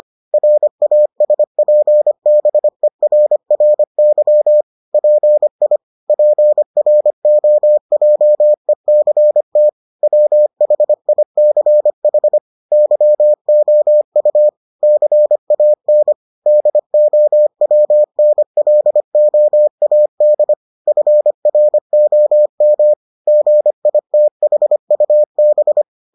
Morse code, heard as a keyed tone, about 25 wpm, 600 Hz. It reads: RASPBERRY PI PROJECT WHICH YOU CAN DOWNLOAD FROM GITHUB